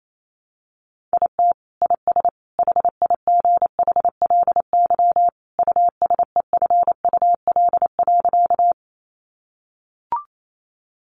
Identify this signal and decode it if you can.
Morse code, heard as a keyed tone, about 28 wpm, 700 Hz. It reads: IT IS HIGHLY USEFUL.
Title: Sentences from Top 1100 Words 28wpm